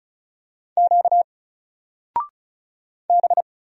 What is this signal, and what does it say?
Morse code, heard as a keyed tone, about 35 wpm, 700 Hz. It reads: Q B
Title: Single Character (Letter:Number:Punctuation:Prosign) 35wpm